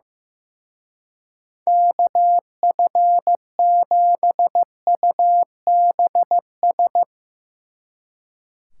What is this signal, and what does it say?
Morse code, heard as a keyed tone, about 16 wpm, 700 Hz. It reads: KF7UBS
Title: Speed Racing - USA 2x3 Call Signs 15wpm